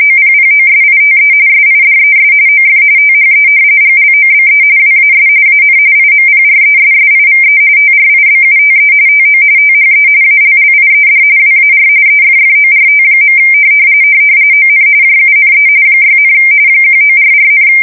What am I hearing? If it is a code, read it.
RTTY, about 45.45 baud. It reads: OPPOSITE SIDES OF THE GLOBE WAS AN IMPRESSIVE TRIBUTE TO THE ALLIED WAR EFFORT. ON JUNE 6, 1944,
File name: fsk170 / battleshipnj